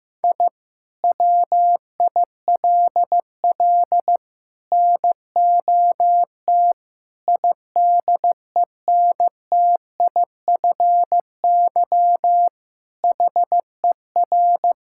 Morse code, heard as a keyed tone, about 15 wpm, 700 Hz. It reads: I WILL NOT IDENTIFY HER
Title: Sentences from Top 1000 Words - Review Part 6 - 15wpm